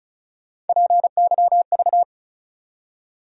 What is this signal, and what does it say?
Morse code, heard as a keyed tone, about 36 wpm, 700 Hz. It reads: PYV
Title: USA Call Sign Postfixes - 3 Letters 35wpm